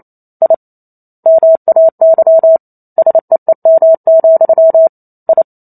Morse code, heard as a keyed tone, about 28 wpm, 650 Hz. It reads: I MAY SEEM, I